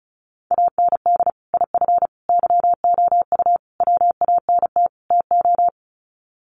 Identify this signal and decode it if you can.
Morse code, heard as a keyed tone, about 34 wpm, 700 Hz. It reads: AND IF YOU WANT TO
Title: Sets of 5 Words Encore 35wpm